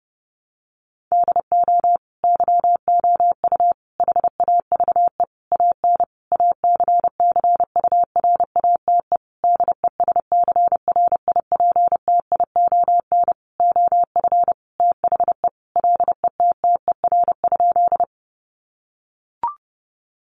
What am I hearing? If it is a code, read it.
Morse code, heard as a keyed tone, about 30 wpm, 700 Hz. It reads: DO YOU HAVE AN ACCURATE DESCRIPTION OF THE LETTER?
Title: Sentences from Top 1300 Words 30wpm